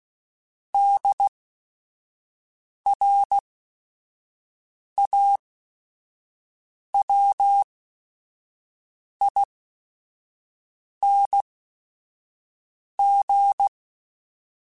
Morse code, heard as a keyed tone, about 16 wpm, spaced as for 5 wpm, 800 Hz. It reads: DRAWING
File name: Special INGs / Special INGs-mp3s / Sngl Ing @5